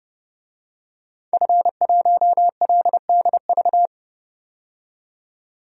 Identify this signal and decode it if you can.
Morse code, heard as a keyed tone, about 30 wpm, 700 Hz. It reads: F1LDV
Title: DXCC-100 Call Signs - 5 Characters - 30wpm